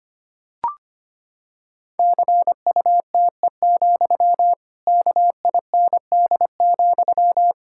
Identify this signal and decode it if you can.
Morse code, heard as a keyed tone, about 25 wpm, 700 Hz. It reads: CUTE, KIND,
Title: Pangram Challenge v2 - 25wpm